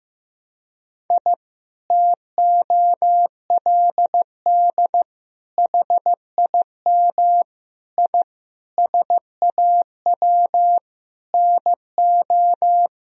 Morse code, heard as a keyed tone, about 15 wpm, 700 Hz. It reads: I TOLD HIM I SAW NO
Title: Sentences from Top 1000 Words - Review Part 1 - 15wpm